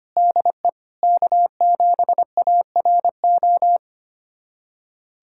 Morse code, heard as a keyed tone, about 25 wpm, 700 Hz. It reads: DE K7ARO
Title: USA Callsign DE Callsign - 5 Characters - 25wpm